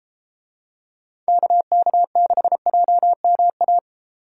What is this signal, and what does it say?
Morse code, heard as a keyed tone, about 31 wpm, 700 Hz. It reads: KK6JMA